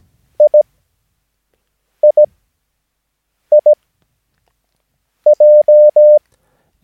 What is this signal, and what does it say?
Morse code, heard as a keyed tone, about 17 wpm, spaced as for 5 wpm, 600 Hz. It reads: IIIJ
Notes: Övningsmaterial med alfabetet A till Ö samt siffror 0 till 9 enligt International/Continental Morse Code.